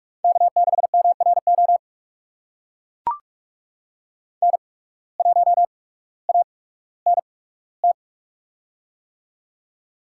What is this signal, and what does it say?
Morse code, heard as a keyed tone, about 44 wpm, spaced as for 19 wpm, 690 Hz. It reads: K6MRO N1ANT
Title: USA Call Signs - 5 Characters - 8x Character Spacing - 45wpm